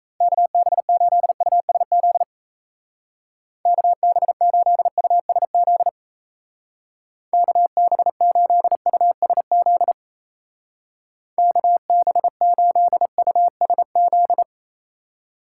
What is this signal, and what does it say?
Morse code, heard as a keyed tone, about 35 wpm, 700 Hz. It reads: KB8USZ KB8USZ KB8USZ KB8USZ
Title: Speed Racing - USA Call Signs 28wpm